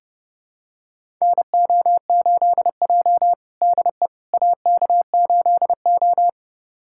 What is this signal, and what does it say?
Morse code, heard as a keyed tone, about 30 wpm, 700 Hz. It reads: NO8J DE AK8O
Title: USA Callsign DE Callsign - 4 Characters - 30wpm